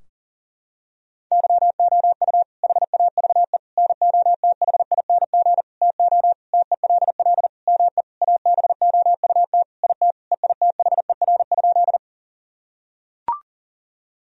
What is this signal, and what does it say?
Morse code, heard as a keyed tone, about 40 wpm, 700 Hz. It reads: YOU HAVE NOTHING TO TELL ME ABOUT IT EITHER?
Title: Sentences from Top 1000 Words - Review Part 4 - 40wpm